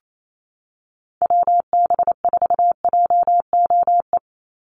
Morse code, heard as a keyed tone, about 28 wpm, 700 Hz. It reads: WB4JOE
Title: USA Call Signs 28wpm